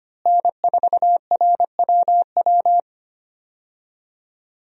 Morse code, heard as a keyed tone, about 25 wpm, 700 Hz. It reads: N4RWW
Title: USA Call Signs - 5 Characters - Repeated 2x - 25wpm